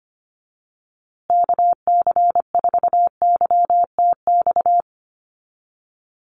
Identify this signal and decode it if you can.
Morse code, heard as a keyed tone, about 25 wpm, 700 Hz. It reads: KC4YTX
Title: USA Call Signs Repeated 3x 25wpm